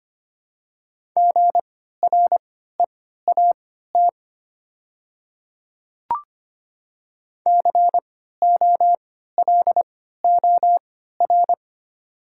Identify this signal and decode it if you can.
Morse code, heard as a keyed tone, about 25 wpm, spaced as for 14 wpm, 700 Hz. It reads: GREAT COLOR
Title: 3x Character Spacing - Top 500 Words - 25wpm